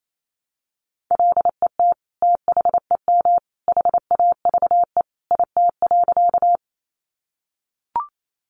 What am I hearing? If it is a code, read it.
Morse code, heard as a keyed tone, about 28 wpm, 700 Hz. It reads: LET THEM HAVE IT.
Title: Sentences from Top 300 Words 28wpm